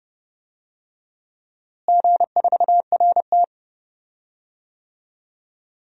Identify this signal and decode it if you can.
Morse code, heard as a keyed tone, about 31 wpm, 700 Hz. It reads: G4RT
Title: DXCC-100 Call Signs - 4 Characters - Repeated 2x - 30wpm